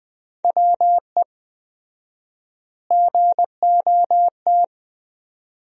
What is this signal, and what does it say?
Morse code, heard as a keyed tone, about 21 wpm, 700 Hz. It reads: WE GOT
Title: 4x Word-Spacing - Sentences from Top 300 Words - 20wpm